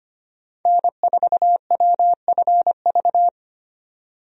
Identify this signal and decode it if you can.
Morse code, heard as a keyed tone, about 25 wpm, 700 Hz. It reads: N4WFV
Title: USA Call Signs - 5 Characters - 25wpm